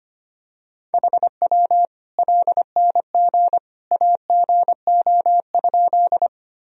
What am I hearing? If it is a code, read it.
Morse code, heard as a keyed tone, about 25 wpm, 700 Hz. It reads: HW LNG AGO?
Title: Sentences from Top 500 Words and Abbreviations 25wpm